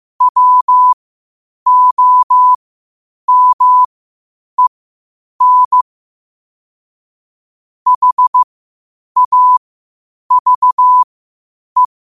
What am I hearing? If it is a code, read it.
Morse code, heard as a keyed tone, about 15 wpm, spaced as for 8 wpm, 1000 Hz. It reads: WOMEN HAVE
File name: qotdfiles / 8wpm